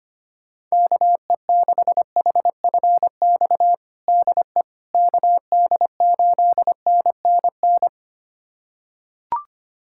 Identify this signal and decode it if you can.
Morse code, heard as a keyed tone, about 25 wpm, 700 Hz. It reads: KE6HFX DE KD8NNN
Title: USA Callsign DE Callsign - 6 Characters - 25wpm